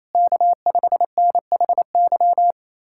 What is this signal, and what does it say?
Morse code, heard as a keyed tone, about 28 wpm, 700 Hz. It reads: K5NHY
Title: USA Callsign DE Callsign - 5 Characters - 28wpm